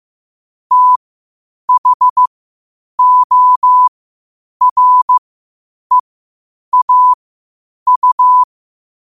Morse code, heard as a keyed tone, about 15 wpm, spaced as for 8 wpm, 1000 Hz. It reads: THOREAU
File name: qotdfiles / 8wpm